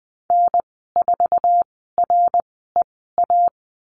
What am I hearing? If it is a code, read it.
Morse code, heard as a keyed tone, about 20 wpm, spaced as for 14 wpm, 700 Hz. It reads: N4REA
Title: 069 Lesson 12g - Letter d - Call Signs 20wpm